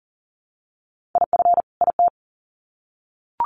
Morse code, heard as a keyed tone, about 40 wpm, 700 Hz. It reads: IF IT
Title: Sets of 2 Words 40wpm